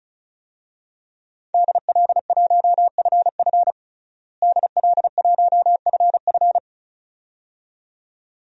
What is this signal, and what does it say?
Morse code, heard as a keyed tone, about 34 wpm, 700 Hz. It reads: DL1FF DL1FF
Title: DXCC-100 Call Signs - 5 Characters - Repeated 2x - 35wpm